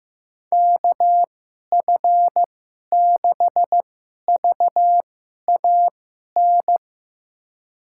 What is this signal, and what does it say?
Morse code, heard as a keyed tone, about 15 wpm, spaced as for 11 wpm, 700 Hz. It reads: KF6VAN
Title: USA Call Signs - 2x Character Spacing - 15wpm